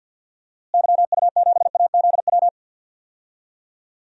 Morse code, heard as a keyed tone, about 53 wpm, 700 Hz. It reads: YU7AZW
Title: DXCC-100 Call Signs - 6 Characters - Repeated 3x - 50wpm